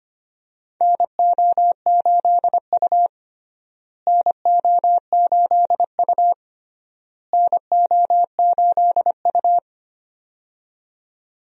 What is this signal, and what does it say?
Morse code, heard as a keyed tone, about 25 wpm, 700 Hz. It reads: NO8U NO8U NO8U
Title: USA Call Signs - 4 Characters - Repeated 3x - 25wpm